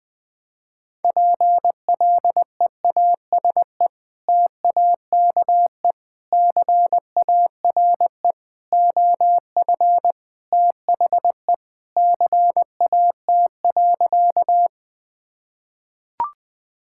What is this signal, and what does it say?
Morse code, heard as a keyed tone, about 20 wpm, 700 Hz. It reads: PLEASE TAKE CARE OF THE CAT.
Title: Sentences with 3-Word Phrases - 20wpm